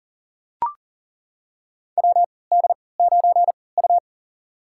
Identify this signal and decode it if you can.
Morse code, heard as a keyed tone, about 37 wpm, spaced as for 22 wpm, 700 Hz. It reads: WD9U
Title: USA Call Signs - 4 Characters - 3x Character Spacing - 40wpm